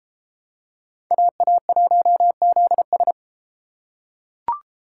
Morse code, heard as a keyed tone, about 33 wpm, 700 Hz. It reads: AA1ZS